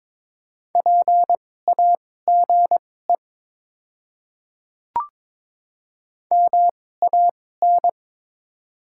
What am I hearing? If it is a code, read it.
Morse code, heard as a keyed tone, about 23 wpm, spaced as for 16 wpm, 700 Hz. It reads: PAGE MAN
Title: 2x Character Spacing - Top 300 Words - 22wpm